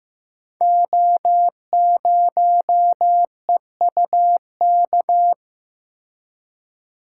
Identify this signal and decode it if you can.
Morse code, heard as a keyed tone, about 15 wpm, 700 Hz. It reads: O0EUK
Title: Code Groups - 5 Letter:Numbers - 15wpm